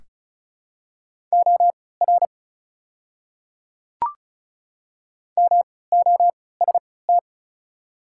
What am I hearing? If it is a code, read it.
Morse code, heard as a keyed tone, about 36 wpm, spaced as for 20 wpm, 700 Hz. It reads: OR MOST
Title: 3x Character Spacing - Top 100 Words - 35wpm